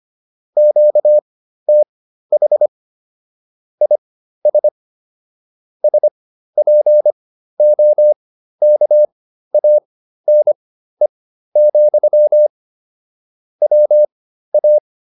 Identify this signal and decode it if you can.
Morse code, heard as a keyed tone, about 25 wpm, spaced as for 13 wpm, 600 Hz. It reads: QTH IS SPOKANE, WA
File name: QSO310 13